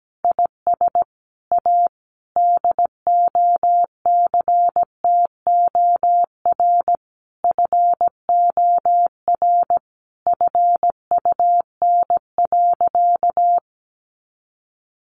Morse code, heard as a keyed tone, about 17 wpm, 700 Hz. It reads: IS A DOCTOR FOR FUN.
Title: Additional Easy Phrases 17wpm